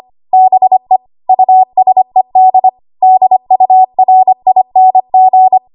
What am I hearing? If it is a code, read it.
Morse code, heard as a keyed tone, about 25 wpm, 750 Hz. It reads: BE USED DURING
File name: Morse / 25WPM